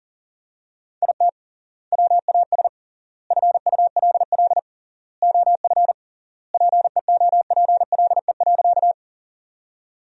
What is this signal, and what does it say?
Morse code, heard as a keyed tone, about 37 wpm, 700 Hz. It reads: IT WAS FULL OF PEOPLE.
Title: 3x Word-Spacing - Sentences from Top 500 Words - 40wpm